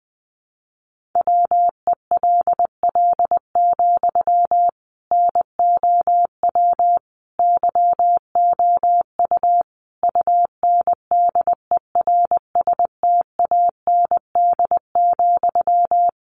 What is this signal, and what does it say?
Morse code, heard as a keyed tone, about 20 wpm, 700 Hz. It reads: WELL, NOW YOU UNDERSTAND,
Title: Sentences from Top 600 Words 20wpm